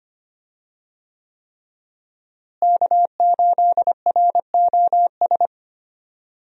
Morse code, heard as a keyed tone, about 25 wpm, 700 Hz. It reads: K8ROS
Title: USA Call Signs - 5 Characters - Repeated 3x - 25wpm